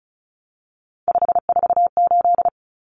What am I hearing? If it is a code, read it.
Morse code, heard as a keyed tone, about 34 wpm, 700 Hz. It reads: 548
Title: Numbers - 3-Digit 35wpm